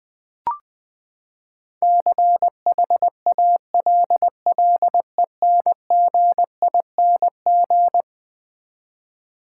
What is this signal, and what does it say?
Morse code, heard as a keyed tone, about 20 wpm, 700 Hz. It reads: CHALLENGING
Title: Eleven Letter Words - 20wpm